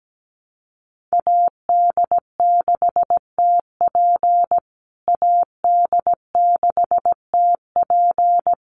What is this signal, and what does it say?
Morse code, heard as a keyed tone, about 17 wpm, 700 Hz. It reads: AD6TP AD6TP
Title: USA Call Signs Repeated 3x 17wpm